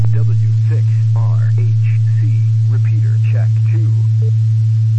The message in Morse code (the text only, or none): none